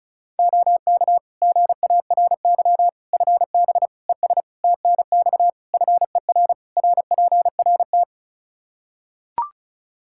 Morse code, heard as a keyed tone, about 35 wpm, 700 Hz. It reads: OK GARY FB ES TNX FER RPRT
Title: Practice QSO - Exchange 2 - Info - 35wpm